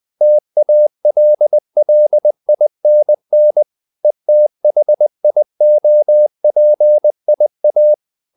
Morse code, heard as a keyed tone, about 20 wpm, 600 Hz. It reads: TALLINN ETHIOPIA
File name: CW / Practice / NationalCapitals20